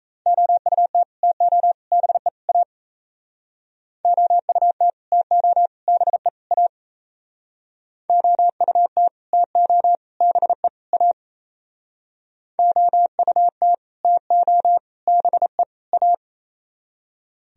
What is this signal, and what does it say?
Morse code, heard as a keyed tone, about 34 wpm, 700 Hz. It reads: OUT TO BE A OUT TO BE A OUT TO BE A OUT TO BE A
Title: Speed-Racing - Sets of 4 Words 28wpm